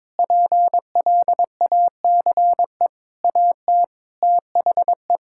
Morse code, heard as a keyed tone, about 21 wpm, 700 Hz. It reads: PLACE AT THE
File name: Sets of 9 Words - 22wpm